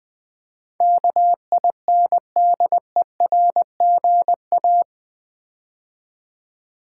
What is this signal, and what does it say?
Morse code, heard as a keyed tone, about 20 wpm, 700 Hz. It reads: KINDERGA
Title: Build A Word - 4 Letters At A Time - 12-Letter Words - 20wpm